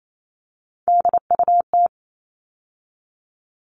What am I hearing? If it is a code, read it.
Morse code, heard as a keyed tone, about 28 wpm, 700 Hz. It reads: DUT
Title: All Cut Numbers - 3-Digit 28wpm